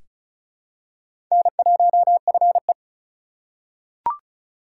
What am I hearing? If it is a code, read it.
Morse code, heard as a keyed tone, about 36 wpm, 700 Hz. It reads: N1FE
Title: K1USN SST QSO Practice - 35wpm